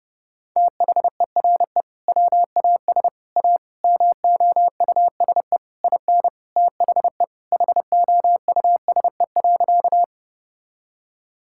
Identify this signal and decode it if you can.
Morse code, heard as a keyed tone, about 30 wpm, 700 Hz. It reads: THERE WAS A MOUSE IN THE HOUSE.
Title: Sentences from Top 1000 Words - Review Part 5 - 30wpm